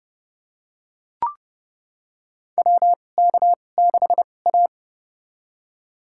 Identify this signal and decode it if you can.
Morse code, heard as a keyed tone, about 31 wpm, spaced as for 22 wpm, 700 Hz. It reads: WK6A
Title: USA Call Signs - 4 Characters - 2x Character Spacing - 30wpm